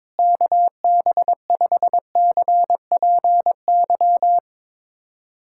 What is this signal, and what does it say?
Morse code, heard as a keyed tone, about 22 wpm, 700 Hz. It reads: KB5CPY
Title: USA Callsign DE Callsign 22wpm